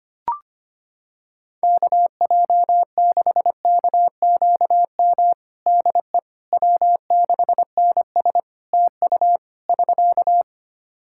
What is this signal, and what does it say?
Morse code, heard as a keyed tone, about 25 wpm, 700 Hz. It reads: KJ6KQM DE W6NS TU <SK>
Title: Practice QSO - Exchange 4 - Ending - 25wpm